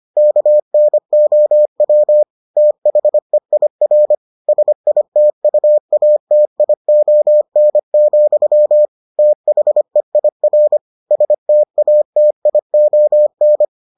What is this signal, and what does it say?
Morse code, heard as a keyed tone, about 25 wpm, 600 Hz. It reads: KNOW THEIR SITUATION, THEIR STATION